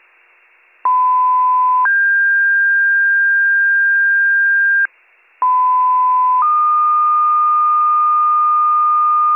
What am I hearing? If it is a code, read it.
Two-tone page, heard as a one-second tone, then a three-second tone, about 1010 then 1600 Hz, about 1010 then 1220 Hz, with noise. It is heard from 0.8 to 9.4 s.